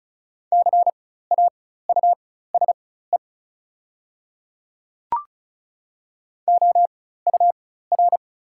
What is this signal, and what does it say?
Morse code, heard as a keyed tone, about 34 wpm, spaced as for 16 wpm, 690 Hz. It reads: CAUSE OUR
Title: 4x Character Spacing - Top 300 Words - 35wpm